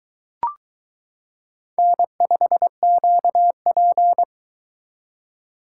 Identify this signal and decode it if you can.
Morse code, heard as a keyed tone, about 22 wpm, 700 Hz. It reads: N5QP